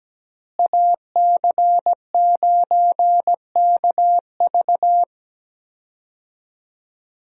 Morse code, heard as a keyed tone, about 17 wpm, 700 Hz. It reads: AC9KV
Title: USA Call Signs - 5 Characters - Repeated 2x - 17wpm